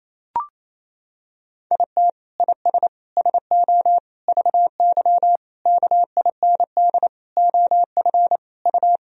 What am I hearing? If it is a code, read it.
Morse code, heard as a keyed tone, about 28 wpm, 700 Hz. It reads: IT IS SO VY KIND OF U
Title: Sentences from Top 300 Words and Abbreviations 28wpm